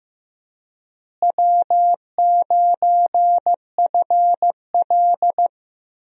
Morse code, heard as a keyed tone, about 15 wpm, 700 Hz. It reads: W9FL
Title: USA Call Signs - 4 Characters - Repeated 2x - 15wpm